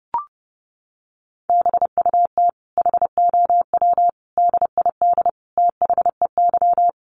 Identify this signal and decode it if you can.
Morse code, heard as a keyed tone, about 30 wpm, 700 Hz. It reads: BUT HOW DID THEY
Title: Sentences from Top 900 Words 30wpm